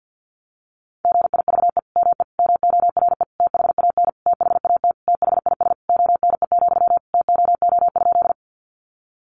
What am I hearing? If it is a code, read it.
Morse code, heard as a keyed tone, about 50 wpm, 700 Hz. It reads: GIVE ME MORE THAN THAT THIS ONE, TOO?
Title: Sentences from Top 200 Words 50wpm